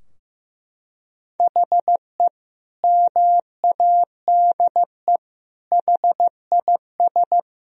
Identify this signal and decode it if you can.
Morse code, heard as a keyed tone, about 15 wpm, 700 Hz. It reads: HE MADE HIS
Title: Sentences from Top 1000 Words - Review Part 4 - 15wpm